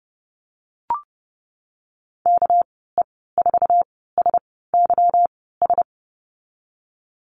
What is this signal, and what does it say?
Morse code, heard as a keyed tone, about 30 wpm, spaced as for 17 wpm, 700 Hz. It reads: KE4SYS
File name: Lesson 37g - Letter x - Call Signs 30wpm